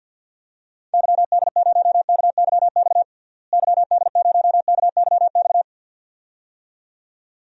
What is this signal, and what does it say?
Morse code, heard as a keyed tone, about 52 wpm, 700 Hz. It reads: YD0KYX YD0KYX
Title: DXCC-100 Call Signs - 6 Characters - Repeated 2x - 50wpm